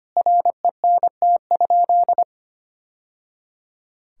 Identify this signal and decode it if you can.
Morse code, heard as a keyed tone, about 25 wpm, 700 Hz. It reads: RENT?
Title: Sentences from Top 1000 Words - Review Part 5 - 25wpm